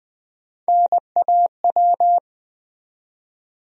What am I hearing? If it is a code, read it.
Morse code, heard as a keyed tone, about 20 wpm, 700 Hz. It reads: NAW
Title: USA Call Sign Postfixes - 3 Letters 20wpm